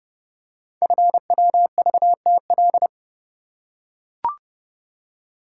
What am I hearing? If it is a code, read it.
Morse code, heard as a keyed tone, about 30 wpm, 700 Hz. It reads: FWVTL
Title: Code Groups - 5 Letter:Numbers - 30wpm